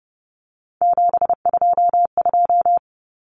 Morse code, heard as a keyed tone, about 30 wpm, 700 Hz. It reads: 722
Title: Common Cut Numbers - 3-Digit 30wpm